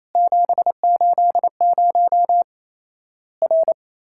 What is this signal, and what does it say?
Morse code, heard as a keyed tone, about 28 wpm, 700 Hz. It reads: 780 R
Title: Worked All Europe DX Contest - QTC Exchange - 28wpm